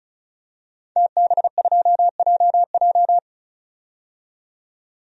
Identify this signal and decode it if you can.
Morse code, heard as a keyed tone, about 34 wpm, 700 Hz. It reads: TB2JJ
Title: DXCC-100 Call Signs - 5 Characters - 35wpm